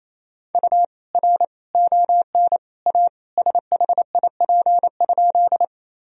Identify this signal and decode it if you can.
Morse code, heard as a keyed tone, about 27 wpm, 700 Hz. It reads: U R ON A SHIP?
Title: Sentences from Top 400 Words and Abbreviations 28wpm